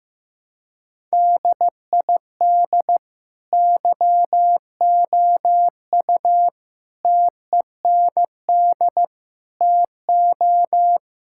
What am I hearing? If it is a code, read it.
Morse code, heard as a keyed tone, about 15 wpm, 700 Hz. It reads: DID YOU TEND TO
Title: Sentences from Top 1000 Words - Review Part 5 - 15wpm